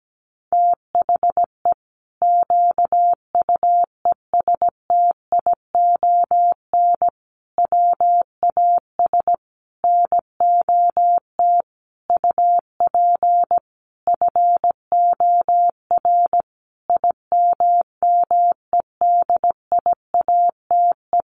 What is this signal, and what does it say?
Morse code, heard as a keyed tone, about 17 wpm, 700 Hz. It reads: THE QUESTION WAS NOT UP FOR IMMEDIATE
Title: Sentences from Top 1200 Words 17wpm